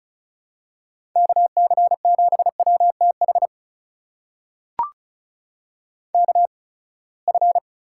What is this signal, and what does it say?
Morse code, heard as a keyed tone, about 36 wpm, spaced as for 24 wpm, 700 Hz. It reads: KC7WTH KF
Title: USA Call Signs - 6 Characters - 8x Character Spacing - 35wpm